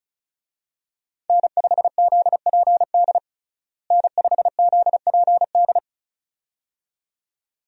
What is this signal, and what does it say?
Morse code, heard as a keyed tone, about 34 wpm, 700 Hz. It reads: N5ZPD N5ZPD
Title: USA Call Signs - 5 Characters - Repeated 2x - 35wpm